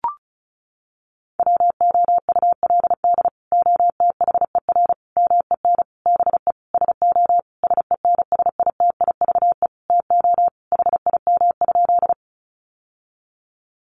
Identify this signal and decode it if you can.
Morse code, heard as a keyed tone, about 34 wpm, 700 Hz. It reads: WOULD OTHER MEN BE SO SENSITIVE TO HIM?
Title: Sentences from Top 1400 Words 35wpm